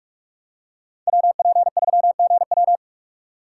Morse code, heard as a keyed tone, about 46 wpm, 700 Hz. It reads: WP3GW
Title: USA Call Signs - 3x Character Spacing - 45wpm